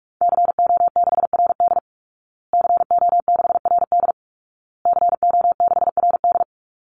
Morse code, heard as a keyed tone, about 44 wpm, 700 Hz. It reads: CO6RD CO6RD CO6RD
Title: Call Signs Repeated 2x - 45wpm